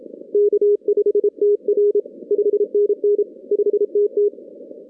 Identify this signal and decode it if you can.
Morse code, heard as a keyed tone, about 29 wpm, 410 Hz. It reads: K5TR 5NN 5TT
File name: 2016 arrl dx cw / 2016-ARRL-DX-CW-K5TR- - 0-46